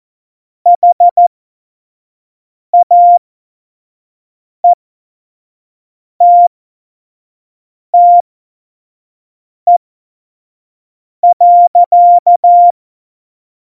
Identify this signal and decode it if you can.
Morse code, heard as a keyed tone, about 14 wpm, spaced as for 5 wpm, 700 Hz. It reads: HAETTE.
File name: Morsekurs / Witze / Witz 21 Juli 2020, 25 F70